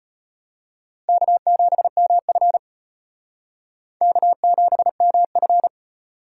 Morse code, heard as a keyed tone, about 32 wpm, 700 Hz. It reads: K7MF K7MF